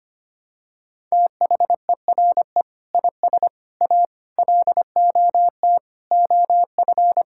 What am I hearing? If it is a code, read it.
Morse code, heard as a keyed tone, about 25 wpm, 700 Hz. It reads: THERE IS A LOT OF
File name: practice / 25wpm / Sentences with 2-Word Phrases - 25wpm